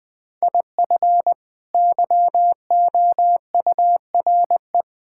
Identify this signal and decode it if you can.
Morse code, heard as a keyed tone, about 20 wpm, 700 Hz. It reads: IF YOURE
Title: Sets of 6 Words - 20wpm